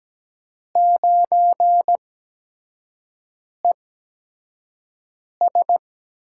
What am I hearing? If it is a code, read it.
Morse code, heard as a keyed tone, about 17 wpm, spaced as for 5 wpm, 700 Hz. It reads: 9ES
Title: USA Call Signs - 5 Characters - 8x Character Spacing - 17wpm